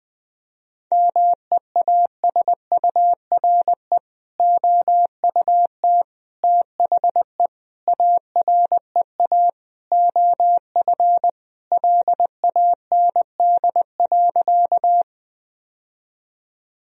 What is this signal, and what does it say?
Morse code, heard as a keyed tone, about 20 wpm, 700 Hz. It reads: MEASURE OUT THE AREA OF LAND.
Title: Sentences from Top 1000 Words - Review Part 2 - 20wpm